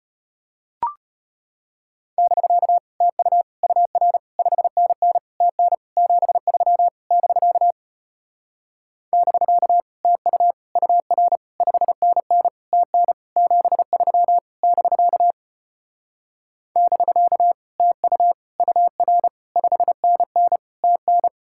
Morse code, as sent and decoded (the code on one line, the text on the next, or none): -...-.- / - ..- / ..- .-. / ..... -. -. / - -. / --... ...-- / -...-.- / -...-.- / - ..- / ..- .-. / ..... -. -. / - -. / --... ...-- / -...-.- / -...-.- / - ..- / ..- .-. / ..... -. -. / - -.
<BK> TU UR 5NN TN 73 <BK> <BK> TU UR 5NN TN 73 <BK> <BK> TU UR 5NN TN